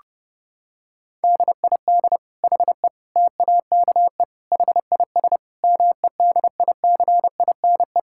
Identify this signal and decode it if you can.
Morse code, heard as a keyed tone, about 32 wpm, 700 Hz. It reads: DID HE TAKE HIS MEDICINE
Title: Sentences from Top 1000 Words - Review Part 6 - 30wpm